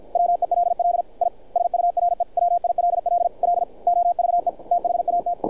Morse code, heard as a keyed tone, about 50 wpm, 700 Hz. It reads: WELL I SURE WILL LOVE THAT
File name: archived / CRSnet